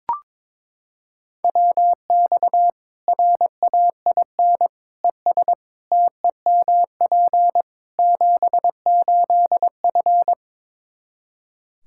Morse code, heard as a keyed tone, about 22 wpm, 700 Hz. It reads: WX RAIN ES TEMP 78F
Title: K1USN SST QSO Practice 22wpm